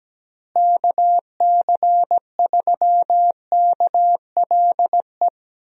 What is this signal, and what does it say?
Morse code, heard as a keyed tone, about 17 wpm, 700 Hz. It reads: KC3KLE
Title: USA Call Signs - 6 Characters - Repeated 3x - 17wpm